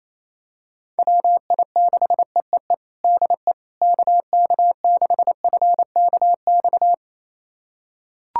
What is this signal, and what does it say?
Morse code, heard as a keyed tone, about 28 wpm, 700 Hz. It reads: WI6EEE DE KK6FKX
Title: USA Callsign DE Callsign - 6 Characters - 28wpm